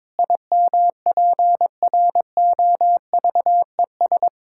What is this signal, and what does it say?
Morse code, heard as a keyed tone, about 22 wpm, 700 Hz. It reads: IMPROVES
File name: Difficult Sentences with Hints - 22wpm